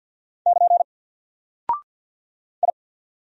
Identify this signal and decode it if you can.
Morse code, heard as a keyed tone, about 48 wpm, 700 Hz. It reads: <KN> I
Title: Single Character (Letter:Number:Punctuation:Prosign) Alt 50wpm